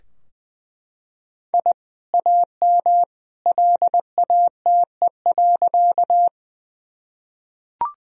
Morse code, heard as a keyed tone, about 20 wpm, 700 Hz. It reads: I AM LATE.
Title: Sentences from Top 1000 Words - Review Part 2 - 20wpm